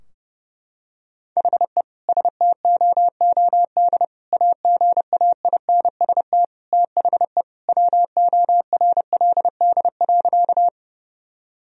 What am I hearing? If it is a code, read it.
Morse code, heard as a keyed tone, about 32 wpm, 700 Hz. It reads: HE STOOD AGAINST THE WORLD.
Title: Sentences from Top 1000 Words - Review Part 2 - 30wpm